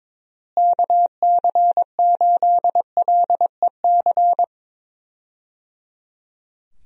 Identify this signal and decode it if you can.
Morse code, heard as a keyed tone, about 22 wpm, 700 Hz. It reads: KC8LEC
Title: USA Call Signs - 6 Characters - Repeated 3x - 22wpm